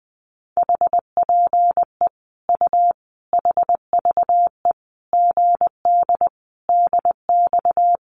Morse code, heard as a keyed tone, about 20 wpm, 700 Hz. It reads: HPE U HVE GD DX
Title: Common QSO Phrases 20wpm